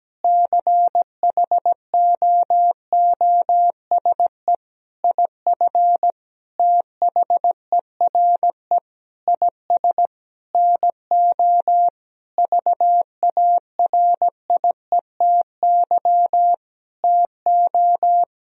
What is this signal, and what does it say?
Morse code, heard as a keyed tone, about 17 wpm, 700 Hz. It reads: CHOOSE IF THERE IS NO VARIETY TO
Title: Sentences from Top 1000 Words - Review Part 4 - 17wpm